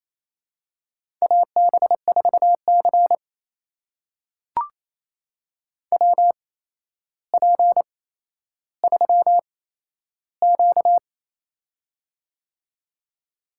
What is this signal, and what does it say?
Morse code, heard as a keyed tone, about 28 wpm, spaced as for 12 wpm, 700 Hz. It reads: AB4C WP3Q
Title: USA Call Signs - 4 Characters - 8x Character Spacing - 28wpm